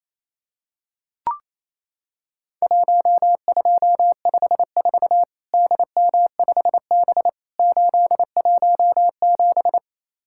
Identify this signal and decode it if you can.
Morse code, heard as a keyed tone, about 28 wpm, 700 Hz. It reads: 1254 DM5B 817
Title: Worked All Europe DX Contest - QTC Exchange - 28wpm